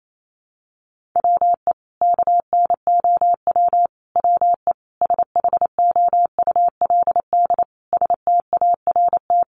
Morse code, heard as a keyed tone, about 28 wpm, 700 Hz. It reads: WE KNOW WE SHOULD START
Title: Sentences from Top 1100 Words 28wpm